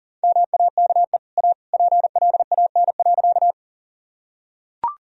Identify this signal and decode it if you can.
Morse code, heard as a keyed tone, about 40 wpm, 700 Hz. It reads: MAKE A PLAN.
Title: Sentences from Top 1000 Words - Review Part 2 - 40wpm